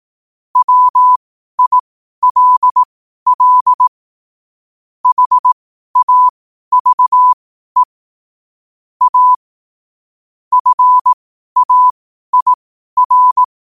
Morse code, heard as a keyed tone, about 18 wpm, spaced as for 12 wpm, 1000 Hz. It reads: WILL HAVE A FAIR
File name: qotdfiles / 12wpm